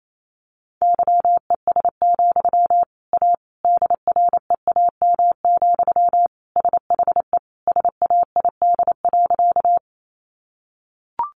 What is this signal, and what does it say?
Morse code, heard as a keyed tone, about 28 wpm, 700 Hz. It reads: YES, A DREAM, SHE SAID.
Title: Sentences from Top 1500 Words 28wpm